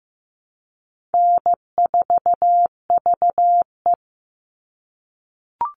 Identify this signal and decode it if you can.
Morse code, heard as a keyed tone, about 15 wpm, 700 Hz. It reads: N4VE
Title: All Cut Numbers - 4-Digit 15wpm